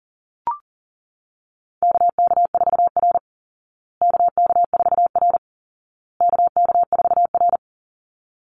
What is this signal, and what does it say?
Morse code, heard as a keyed tone, about 40 wpm, 700 Hz. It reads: KK4R KK4R KK4R
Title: Call Signs Repeated 2x - 40wpm